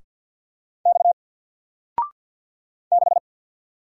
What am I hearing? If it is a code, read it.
Morse code, heard as a keyed tone, about 50 wpm, 700 Hz. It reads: X 6
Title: Single Letter-Number Alt 50wpm